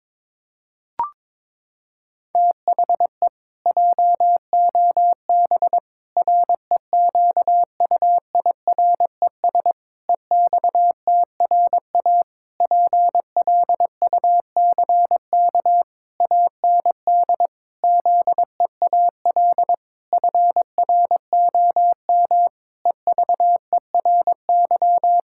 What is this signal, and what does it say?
Morse code, heard as a keyed tone, about 22 wpm, 700 Hz. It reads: THE JOB REQUIRES EXTRA PLUCK AND ZEAL FROM EVERY
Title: Pangram Challenge v2 - 22wpm